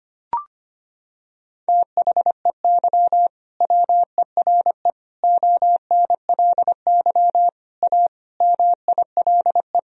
Morse code, heard as a keyed tone, about 25 wpm, 700 Hz. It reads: THEY WERE ONLY A MILE
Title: Sentences from Top 1000 Words - Review Part 1 - 25wpm